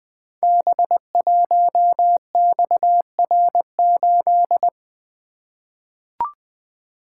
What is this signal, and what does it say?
Morse code, heard as a keyed tone, about 20 wpm, 700 Hz. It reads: B1XR8
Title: Code Groups - 5 Letter:Numbers - 20wpm